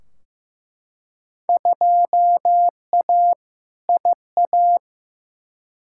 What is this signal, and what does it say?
Morse code, heard as a keyed tone, about 15 wpm, 700 Hz. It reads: 2A IA
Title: Field Day Exchanges Encore - 15wpm